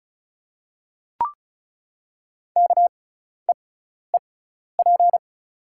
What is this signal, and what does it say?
Morse code, heard as a keyed tone, about 34 wpm, spaced as for 12 wpm, 700 Hz. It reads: KEEP
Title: 6x Character Spacing - Top 300 Words - 35wpm